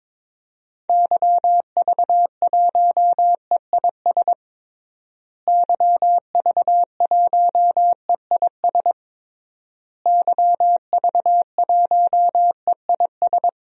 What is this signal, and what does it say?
Morse code, heard as a keyed tone, about 22 wpm, 700 Hz. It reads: YV1EIS YV1EIS YV1EIS
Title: DXCC-100 Call Signs - 6 Characters - Repeated 3x - 22wpm